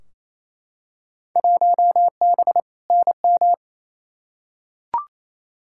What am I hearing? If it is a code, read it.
Morse code, heard as a keyed tone, about 28 wpm, 700 Hz. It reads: 1B NM
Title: Field Day Exchanges Encore - 28wpm